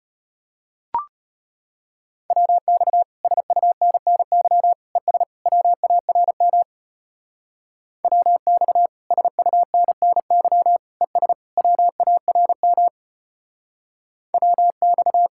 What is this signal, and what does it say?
Morse code, heard as a keyed tone, about 32 wpm, 700 Hz. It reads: WX SUNNY ES WARM WX SUNNY ES WARM WX